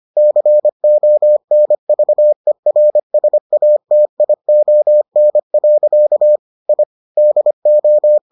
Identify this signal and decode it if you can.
Morse code, heard as a keyed tone, about 25 wpm, 600 Hz. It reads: CONVERSATION. I DO